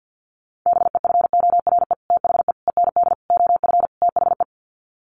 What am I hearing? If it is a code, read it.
Morse code, heard as a keyed tone, about 50 wpm, 700 Hz. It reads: BEFORE THE END OF THE
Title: Sets of 5 Words Encore 50wpm